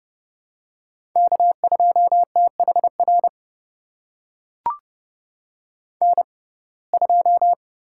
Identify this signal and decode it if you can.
Morse code, heard as a keyed tone, about 30 wpm, spaced as for 22 wpm, 690 Hz. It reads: K2THR N2
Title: USA Call Signs - 5 Characters - 6x Character Spacing - 30wpm